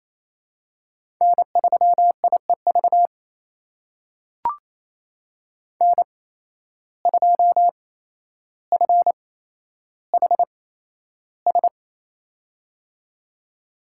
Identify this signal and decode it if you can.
Morse code, heard as a keyed tone, about 28 wpm, spaced as for 12 wpm, 700 Hz. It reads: N3IEV N2FHS
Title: USA Call Signs - 5 Characters - 8x Character Spacing - 28wpm